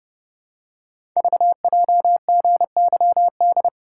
Morse code, heard as a keyed tone, about 30 wpm, 700 Hz. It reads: VJGYD
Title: Code Groups - 5 Letters - 30wpm